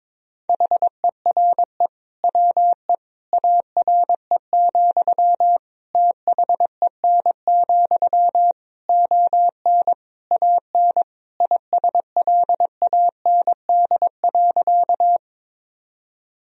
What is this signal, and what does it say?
Morse code, heard as a keyed tone, about 22 wpm, 700 Hz. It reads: HERE WE ARE, THEN, ON AN ISLAND.
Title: Sentences from Top 1000 Words - Review Part 3 - 22wpm